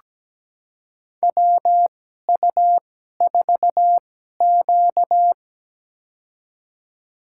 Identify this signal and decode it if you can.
Morse code, heard as a keyed tone, about 17 wpm, spaced as for 12 wpm, 700 Hz. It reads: WU4Q
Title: USA Call Signs - 4 Characters - 2x Character Spacing - 17wpm